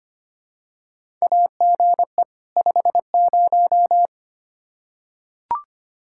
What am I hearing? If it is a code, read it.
Morse code, heard as a keyed tone, about 25 wpm, 700 Hz. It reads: AGE 50
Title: K1USN SST QSO Practice 25wpm